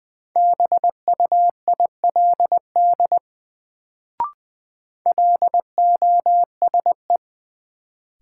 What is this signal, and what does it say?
Morse code, heard as a keyed tone, about 20 wpm, 700 Hz. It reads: BUILD LOSE
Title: Sentences from Top 1100 Words - New Word Drill Alt - 20wpm